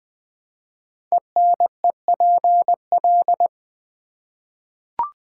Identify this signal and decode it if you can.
Morse code, heard as a keyed tone, about 20 wpm, 700 Hz. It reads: ENEPL
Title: Code Groups - 5 Letters - 20wpm